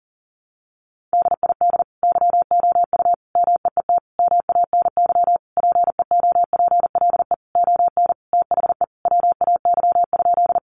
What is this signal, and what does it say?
Morse code, heard as a keyed tone, about 40 wpm, 700 Hz. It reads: DID YOU MEET MANY PEOPLE ON THE WAY?
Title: Sentences from Top 1100 Words 40wpm